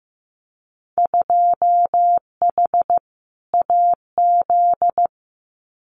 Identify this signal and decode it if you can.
Morse code, heard as a keyed tone, about 15 wpm, 700 Hz. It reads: 2H AZ
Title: Winter Field Day Exchanges 15wpm